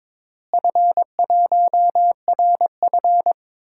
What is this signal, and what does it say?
Morse code, heard as a keyed tone, about 22 wpm, 700 Hz. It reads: F1RF
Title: DXCC-100 Call Signs - 4 Characters - Repeated 2x - 22wpm